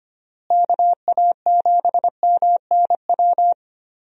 Morse code, heard as a keyed tone, about 25 wpm, 700 Hz. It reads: KA7MNW
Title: USA Call Signs - 6 Characters - Repeated 3x - 25wpm